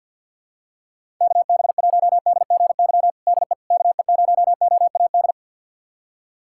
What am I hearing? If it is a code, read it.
Morse code, heard as a keyed tone, about 47 wpm, 700 Hz. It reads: KB1DGY DE KE0OAD
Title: USA Callsign DE Callsign - 6 Characters - 50wpm